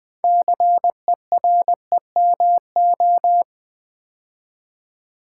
Morse code, heard as a keyed tone, about 21 wpm, 700 Hz. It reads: CEREMO
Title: Build A Word - 3 Letters At A Time - 8-Letter Words - 20wpm